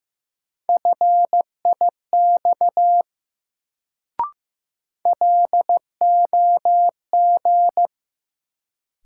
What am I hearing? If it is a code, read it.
Morse code, heard as a keyed tone, about 15 wpm, 700 Hz. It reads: FIX LOG
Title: Sentences from Top 1400 Words - New Word Drill Alt - 15wpm